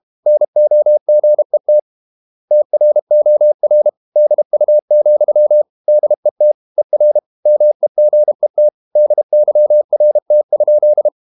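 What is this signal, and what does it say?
Morse code, heard as a keyed tone, about 32 wpm, 600 Hz. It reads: NOGET TROR DU, DET ER MEGET DYRT?
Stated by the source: High-speed QRQ CW at 32 WPM.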